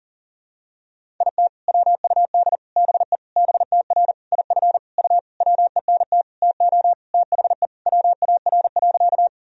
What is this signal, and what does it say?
Morse code, heard as a keyed tone, about 40 wpm, 700 Hz. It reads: IT WUD BE BTR IF U WENT TO THE WAR.
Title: Sentences from Top 400 Words and Abbreviations 40wpm